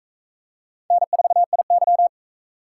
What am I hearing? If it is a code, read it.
Morse code, heard as a keyed tone, about 44 wpm, 700 Hz. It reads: N4IY